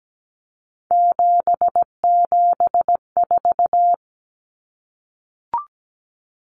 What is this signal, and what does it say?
Morse code, heard as a keyed tone, about 17 wpm, 700 Hz. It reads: 774
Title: Common Cut Numbers - 3-Digit 17wpm